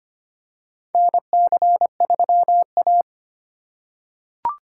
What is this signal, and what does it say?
Morse code, heard as a keyed tone, about 25 wpm, 700 Hz. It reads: NC3A
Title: USA Call Signs - 4 Characters - 3x Character Spacing - 25wpm